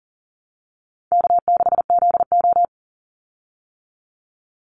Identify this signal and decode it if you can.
Morse code, heard as a keyed tone, about 40 wpm, 700 Hz. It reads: K6ZO
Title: Easy Call Signs - 40wpm